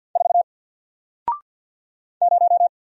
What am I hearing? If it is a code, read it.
Morse code, heard as a keyed tone, about 48 wpm, 700 Hz. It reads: 4 0
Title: Single Character (Letter:Number:Punctuation:Prosign) Alt 50wpm